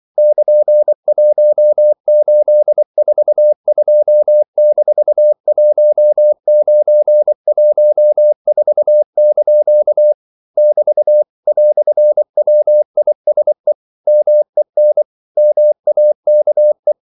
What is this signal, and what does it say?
Morse code, heard as a keyed tone, about 24 wpm, 600 Hz. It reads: <KN>1842-1914) <BT> "WISE MEN MAKE
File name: CW / Practice / EnglishQuotations0224